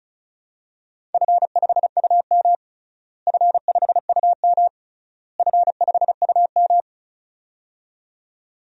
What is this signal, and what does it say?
Morse code, heard as a keyed tone, about 36 wpm, 700 Hz. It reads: F5UM F5UM F5UM
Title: DXCC-100 Call Signs - 4 Characters - Repeated 3x - 35wpm